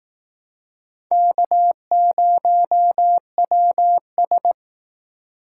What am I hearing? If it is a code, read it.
Morse code, heard as a keyed tone, about 18 wpm, 700 Hz. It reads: K0WS
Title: Speed Racing - USA 1x2 Call Signs 15wpm